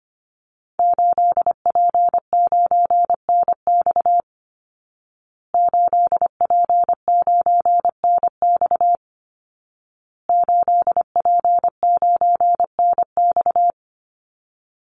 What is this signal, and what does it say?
Morse code, heard as a keyed tone, about 25 wpm, 700 Hz. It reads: 8P9NX 8P9NX 8P9NX
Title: Call Signs Repeated 2x - 25wpm